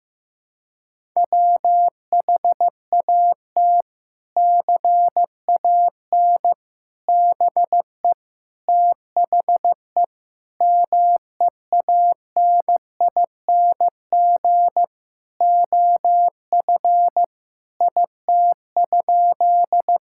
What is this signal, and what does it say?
Morse code, heard as a keyed tone, about 15 wpm, 700 Hz. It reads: WHAT CAN BE THE MEANING OF IT?
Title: Sentences from Top 1000 Words - Review Part 6 - 15wpm